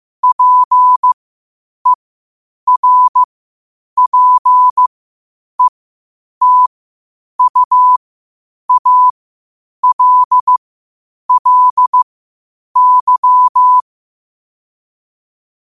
Morse code, heard as a keyed tone, about 15 wpm, spaced as for 8 wpm, 1000 Hz. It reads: PERPETUALLY